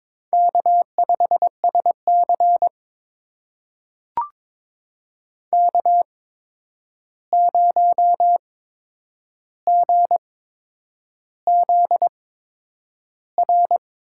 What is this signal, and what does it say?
Morse code, heard as a keyed tone, about 22 wpm, spaced as for 9 wpm, 700 Hz. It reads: K5SC K0GZR
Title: USA Call Signs - 8x Character Spacing - 22wpm